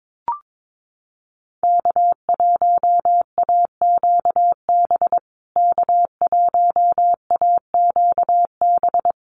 Morse code, heard as a keyed tone, about 22 wpm, 700 Hz. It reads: K1AQB K1AQB
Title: USA Call Signs Repeated 3x 22wpm